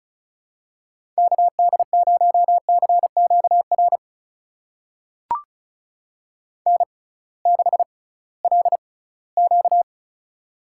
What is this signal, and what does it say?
Morse code, heard as a keyed tone, about 36 wpm, spaced as for 20 wpm, 700 Hz. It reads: KD0CQR N6LQ
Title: USA Call Signs - 6x Character Spacing - 35wpm